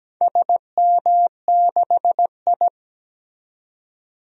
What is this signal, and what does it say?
Morse code, heard as a keyed tone, about 17 wpm, 700 Hz. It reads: SM6I
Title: DXCC-100 Call Signs - 4 Characters - 17wpm